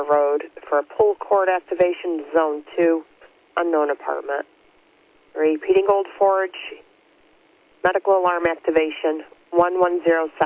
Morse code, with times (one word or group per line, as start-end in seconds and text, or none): none